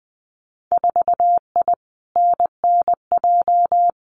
Morse code, heard as a keyed tone, about 20 wpm, 700 Hz. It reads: 4I NNJ
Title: Winter Field Day Exchanges 20wpm